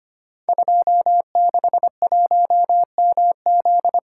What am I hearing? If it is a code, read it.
Morse code, heard as a keyed tone, about 25 wpm, 700 Hz. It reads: 261MZ
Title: Code Groups - 5 Letter:Numbers - 25wpm